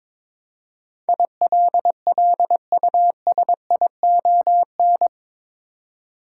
Morse code, heard as a keyed tone, about 22 wpm, 700 Hz. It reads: ILLUSION
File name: practice / 22wpm / Eight Letter Words - 22wpm